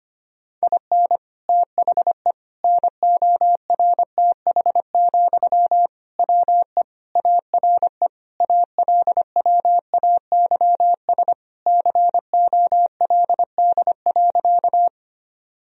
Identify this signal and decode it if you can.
Morse code, heard as a keyed tone, about 25 wpm, 700 Hz. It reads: IN THE NORTH, WE ARE ALWAYS COLD.
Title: Sentences from Top 1000 Words - Review Part 1 - 25wpm